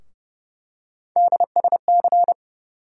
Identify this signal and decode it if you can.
Morse code, heard as a keyed tone, about 32 wpm, 700 Hz. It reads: DSC
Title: USA Call Sign Postfixes - 3 Letters 30wpm